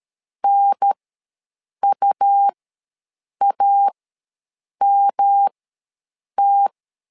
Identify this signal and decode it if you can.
Morse code, heard as a keyed tone, about 13 wpm, spaced as for 7 wpm, 800 Hz. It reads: NUAMT
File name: LEZIONI / 3 MWRO / 3 MW ws 12wpm 1816